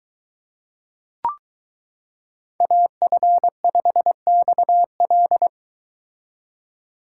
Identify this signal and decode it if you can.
Morse code, heard as a keyed tone, about 23 wpm, 700 Hz. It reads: AF5XL